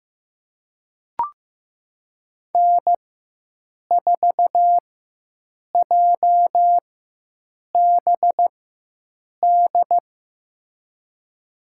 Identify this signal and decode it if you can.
Morse code, heard as a keyed tone, about 15 wpm, spaced as for 7 wpm, 700 Hz. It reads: N4JBD
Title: USA Call Signs - 5 Characters - 4x Character Spacing - 15wpm